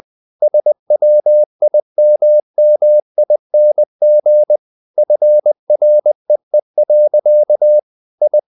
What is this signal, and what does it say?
Morse code, heard as a keyed tone, about 20 wpm, 600 Hz. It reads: SWIMMING FREE. I
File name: the old man and the sea part 2 20wpm